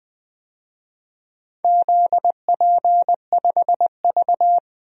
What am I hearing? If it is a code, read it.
Morse code, heard as a keyed tone, about 20 wpm, 700 Hz. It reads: ZP5V
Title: DXCC-100 Call Signs - 4 Characters - 20wpm